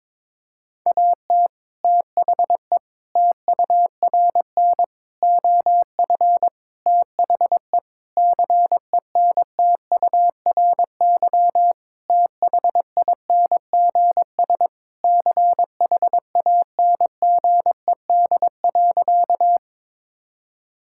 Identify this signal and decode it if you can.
Morse code, heard as a keyed tone, about 22 wpm, 700 Hz. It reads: AT THE TURN OF THE CENTURY THINGS CHANGED.
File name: practice / 22wpm / Sentences with 4-Word Phrases - 22wpm